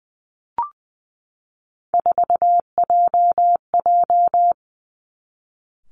Morse code, heard as a keyed tone, about 20 wpm, 700 Hz. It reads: 4JJ
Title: Three Challenging Letter-Number Combinations 20wpm